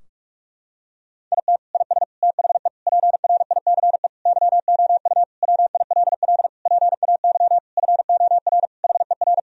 Morse code, heard as a keyed tone, about 45 wpm, 700 Hz. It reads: IT IS THE PRICE YOU WILL PAY FOR HER
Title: Sentences from Top 1000 Words - Review Part 4 - 45wpm